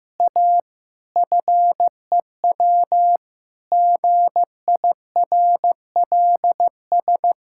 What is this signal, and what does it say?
Morse code, heard as a keyed tone, about 15 wpm, 700 Hz. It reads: A FEW GIRLS
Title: Pangram Challenge v2 - 15wpm